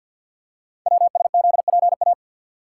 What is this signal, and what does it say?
Morse code, heard as a keyed tone, about 50 wpm, 690 Hz. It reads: WSZPA
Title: Code Groups - 5 Letters - 50wpm